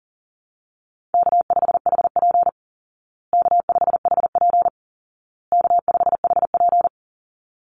Morse code, heard as a keyed tone, about 40 wpm, 700 Hz. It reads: K5HP K5HP K5HP
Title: Easy Call Signs Repeated 3x - 40wpm